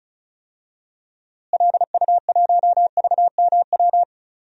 Morse code, heard as a keyed tone, about 34 wpm, 700 Hz. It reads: LU1VMW
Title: DXCC-100 Call Signs - 6 Characters - Repeated 3x - 35wpm